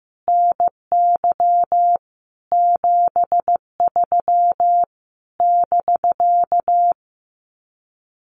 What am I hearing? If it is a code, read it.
Morse code, heard as a keyed tone, about 15 wpm, 700 Hz. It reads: NY 73 <BK>
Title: Common QSO Phrases Encore 15wpm